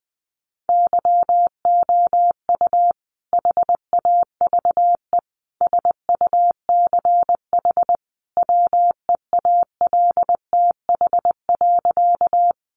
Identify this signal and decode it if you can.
Morse code, heard as a keyed tone, about 20 wpm, 700 Hz. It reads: YOU HAVE SUCH WEALTH.
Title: Sentences from Top 1200 Words 20wpm